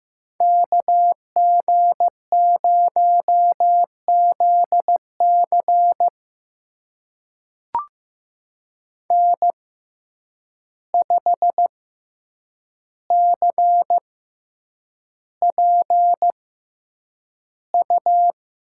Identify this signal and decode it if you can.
Morse code, heard as a keyed tone, about 15 wpm, spaced as for 8 wpm, 700 Hz. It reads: KG0ZC N5CPU
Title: USA Call Signs - 5 Characters - 6x Character Spacing - 15wpm